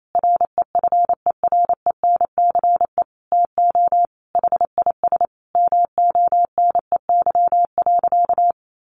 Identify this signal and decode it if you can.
Morse code, heard as a keyed tone, about 28 wpm, 700 Hz. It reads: REFERENCE TO HIS MONEY.
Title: Sentences from Top 1200 Words 28wpm